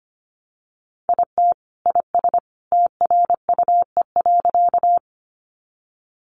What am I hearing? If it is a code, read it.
Morse code, heard as a keyed tone, about 25 wpm, 700 Hz. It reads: IT IS TRUE.
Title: Sentences from Top 400 Words 25wpm